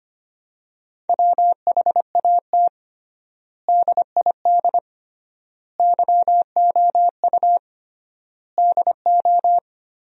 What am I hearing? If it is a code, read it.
Morse code, heard as a keyed tone, about 25 wpm, 700 Hz. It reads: WHAT DID YOU DO
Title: 3x Word-Spacing - Sentences from Top 100 Words - 25wpm